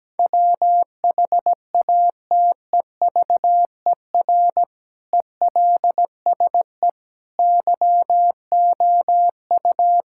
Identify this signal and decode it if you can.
Morse code, heard as a keyed tone, about 17 wpm, 700 Hz. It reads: WHATEVER ELSE YOU
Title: Sentences from Top 1600 Words 17wpm